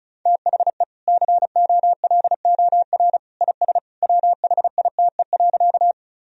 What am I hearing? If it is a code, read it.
Morse code, heard as a keyed tone, about 36 wpm, 700 Hz. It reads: THE COLOR IS WHITE.
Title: Sentences from Top 1000 Words - Review Part 2 - 35wpm